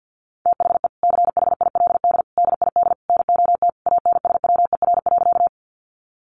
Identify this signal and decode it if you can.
Morse code, heard as a keyed tone, about 50 wpm, 700 Hz. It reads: THE CHILD DID NOT ANSWER.
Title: Sentences from Top 800 Words 50wpm